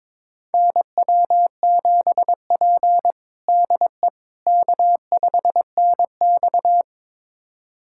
Morse code, heard as a keyed tone, about 22 wpm, 700 Hz. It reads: NW7P DE K5NX
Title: USA Callsign DE Callsign - 4 Characters - 22wpm